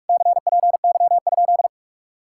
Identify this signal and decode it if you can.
Morse code, heard as a keyed tone, about 57 wpm, 700 Hz. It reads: CPY?